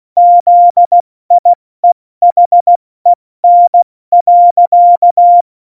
Morse code, heard as a keyed tone, about 16 wpm, spaced as for 14 wpm, 700 Hz. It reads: ZIEHEN.
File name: Morsekurs / Witze / Witz 06. November 2018, 70 F80